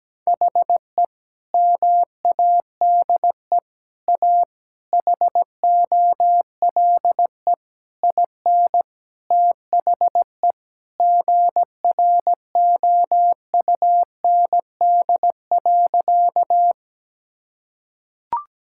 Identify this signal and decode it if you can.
Morse code, heard as a keyed tone, about 17 wpm, 700 Hz. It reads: HE MADE A HOLE IN THE GROUND.
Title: Sentences from Top 1600 Words 17wpm